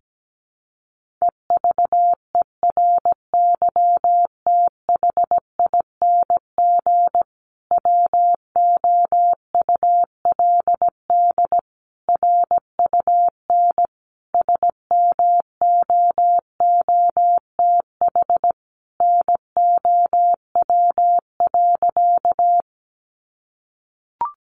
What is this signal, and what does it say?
Morse code, heard as a keyed tone, about 17 wpm, 700 Hz. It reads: EVERYTHING WOULD RUN SMOOTH NOW.
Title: Sentences from Top 1400 Words 17wpm